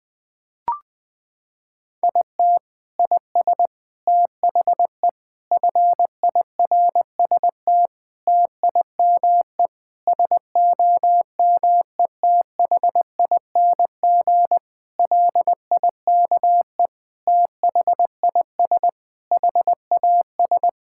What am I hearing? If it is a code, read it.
Morse code, heard as a keyed tone, about 20 wpm, 700 Hz. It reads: IT IS THE FIRST TIME SOMETHING LIKE THIS HAS
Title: Sentences with 8-Word Phrases - 20wpm